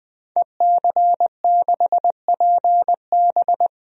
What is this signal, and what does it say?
Morse code, heard as a keyed tone, about 21 wpm, 700 Hz. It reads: EC6PB
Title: DXCC-100 Call Signs - 5 Characters - 20wpm